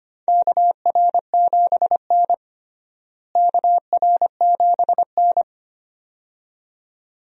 Morse code, heard as a keyed tone, about 25 wpm, 700 Hz. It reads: KR7N KR7N
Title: USA Call Signs - 4 Characters - Repeated 2x - 25wpm